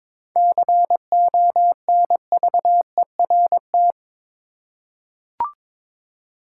Morse code, heard as a keyed tone, about 22 wpm, 700 Hz. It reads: CONVERT
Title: Seven Letter Words - 22wpm